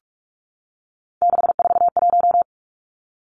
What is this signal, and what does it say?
Morse code, heard as a keyed tone, about 46 wpm, 700 Hz. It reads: 641
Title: Three Challenging Letter-Number Combinations 45wpm